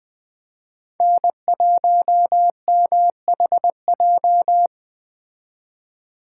Morse code, heard as a keyed tone, about 20 wpm, 700 Hz. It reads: N1MHJ
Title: Speed Racing - USA Call Signs 17wpm